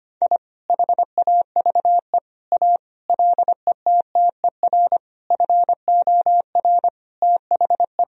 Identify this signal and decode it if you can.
Morse code, heard as a keyed tone, about 25 wpm, 700 Hz. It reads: I HAVE A LETTER FOR THE
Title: Sentences from Top 1000 Words - Review Part 1 - 25wpm